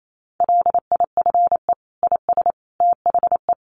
Morse code, heard as a keyed tone, about 28 wpm, 700 Hz. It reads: LIFE IS THE
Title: Sentences from Top 900 Words 28wpm